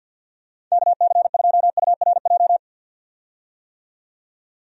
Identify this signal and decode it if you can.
Morse code, heard as a keyed tone, about 52 wpm, 700 Hz. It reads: KC2URJ
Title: USA Call Signs - 6 Characters - 50wpm